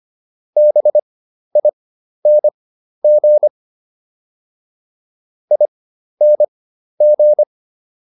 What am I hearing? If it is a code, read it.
Morse code, heard as a keyed tone, about 25 wpm, spaced as for 12 wpm, 600 Hz. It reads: BING ING